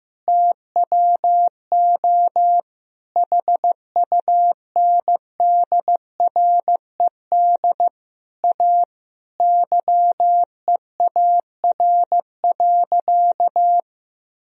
Morse code, heard as a keyed tone, about 15 wpm, 700 Hz. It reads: TWO HUNDRED A YEAR.
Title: Sentences from Top 1000 Words - Review Part 2 - 15wpm